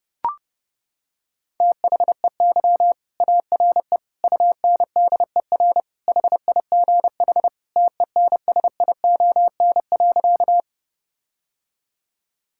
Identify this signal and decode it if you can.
Morse code, heard as a keyed tone, about 30 wpm, 700 Hz. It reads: THEY ARE UNDER HIGH TENSION.
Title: Sentences from Top 1600 Words 30wpm